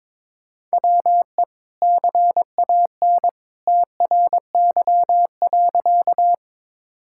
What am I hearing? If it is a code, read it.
Morse code, heard as a keyed tone, about 22 wpm, 700 Hz. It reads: WE CAN TRY.
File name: practice / 22wpm / Sentences with 2-Word Phrases - 22wpm